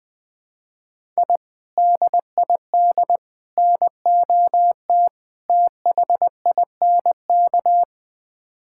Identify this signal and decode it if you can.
Morse code, heard as a keyed tone, about 20 wpm, 700 Hz. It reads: I DID NOT THINK
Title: Build A Phrase - 4-Word Phrases - 20wpm